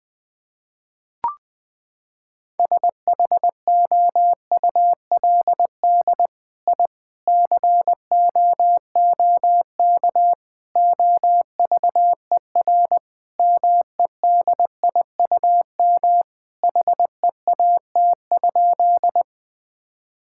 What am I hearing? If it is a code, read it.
Morse code, heard as a keyed tone, about 21 wpm, 700 Hz. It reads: SHOULD I COOK OVER MEDIUM HEAT?
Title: Sentences from Top 1000 Words - Review Part 6 - 20wpm